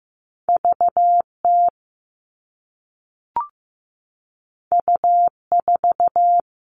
Morse code, heard as a keyed tone, about 15 wpm, 700 Hz. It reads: VT U4
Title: All Cut Numbers - 2-Digit 15wpm